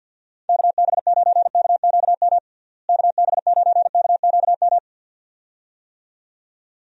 Morse code, heard as a keyed tone, about 51 wpm, 700 Hz. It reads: KB9KQM KB9KQM
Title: USA Call Signs - 6 Characters - Repeated 2x - 50wpm